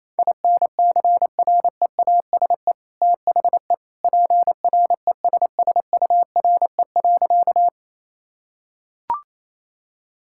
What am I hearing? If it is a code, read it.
Morse code, heard as a keyed tone, about 28 wpm, 700 Hz. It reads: INCREASE THE PRESSURE.
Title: Sentences from Top 1000 Words - Review Part 6 - 28wpm